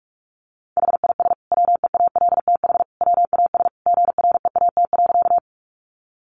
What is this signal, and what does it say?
Morse code, heard as a keyed tone, about 44 wpm, 700 Hz. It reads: HIS WEALTH WAS GREAT.
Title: Sentences from Top 1200 Words 45wpm